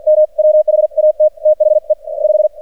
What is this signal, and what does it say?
Morse code, heard as a keyed tone, about 48 wpm, 610 Hz, with radio noise. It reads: M WHAT THE 5
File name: archived / CRSnet